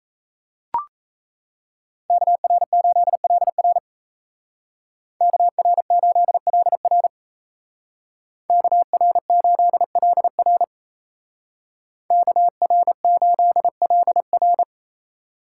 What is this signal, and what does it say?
Morse code, heard as a keyed tone, about 34 wpm, 700 Hz. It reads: KR8LR KR8LR KR8LR KR8LR
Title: Speed Racing - USA 2x2 Call Signs 28wpm